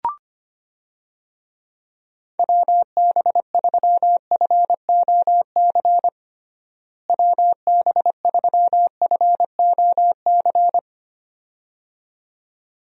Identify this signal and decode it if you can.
Morse code, heard as a keyed tone, about 25 wpm, 700 Hz. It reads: WB3FOC WB3FOC
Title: USA Call Signs - 6 Characters - Repeated 2x - 25wpm